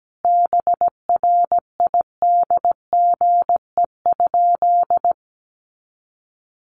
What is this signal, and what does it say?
Morse code, heard as a keyed tone, about 17 wpm, 700 Hz. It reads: BRIDGE?
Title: Sentences from Top 1200 Words 17wpm